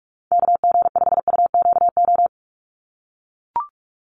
Morse code, heard as a keyed tone, about 44 wpm, 700 Hz. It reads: KG5UQO
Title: USA Call Signs 45wpm